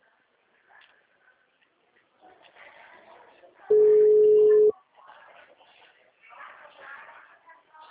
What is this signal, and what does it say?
Ringback tone (Europe, 425 Hz), with speech.
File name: out- -2032- - - .48